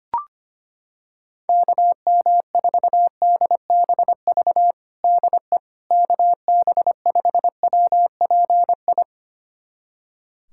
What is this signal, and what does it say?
Morse code, heard as a keyed tone, about 25 wpm, 700 Hz. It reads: KM4DBV DE KB5WPI
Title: USA Callsign DE Callsign - 6 Characters - 25wpm